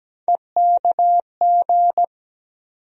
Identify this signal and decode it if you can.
Morse code, heard as a keyed tone, about 17 wpm, 700 Hz. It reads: EKG
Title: USA Call Sign Postfixes - 3 Letters 17wpm